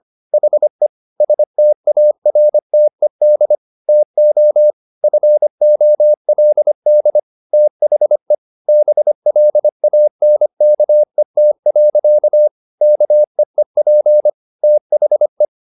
Morse code, heard as a keyed tone, about 25 wpm, 600 Hz. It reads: HE STARTED TO FOLD THE BLANKET. KEEP THE
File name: the old man and the sea part 1 25wpm